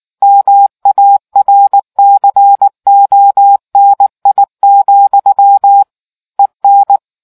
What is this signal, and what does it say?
Morse code, heard as a keyed tone, about 19 wpm, 800 Hz. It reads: MARCONI, EN